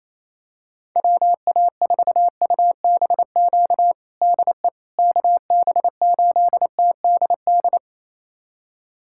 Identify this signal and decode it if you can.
Morse code, heard as a keyed tone, about 28 wpm, 700 Hz. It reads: WA4UBQ DE KB8TDD
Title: USA Callsign DE Callsign - 6 Characters - 28wpm